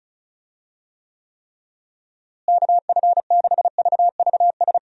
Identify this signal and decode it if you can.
Morse code, heard as a keyed tone, about 36 wpm, 700 Hz. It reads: KF6VVS
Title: USA Call Signs - 6 Characters - Repeated 3x - 35wpm